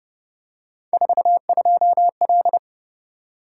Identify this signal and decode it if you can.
Morse code, heard as a keyed tone, about 28 wpm, 700 Hz. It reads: 42L